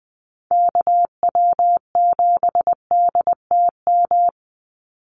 Morse code, heard as a keyed tone, about 20 wpm, 700 Hz. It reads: KW7DTM
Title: USA Call Signs 20wpm